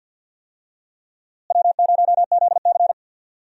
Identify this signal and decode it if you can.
Morse code, heard as a keyed tone, about 52 wpm, 700 Hz. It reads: W0ZC
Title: USA Call Signs - 4 Characters - Repeated 3x - 50wpm